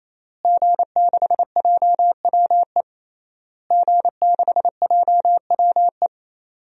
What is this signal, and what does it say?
Morse code, heard as a keyed tone, about 28 wpm, 700 Hz. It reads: G6JWE G6JWE
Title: DXCC-100 Call Signs - 5 Characters - Repeated 3x - 28wpm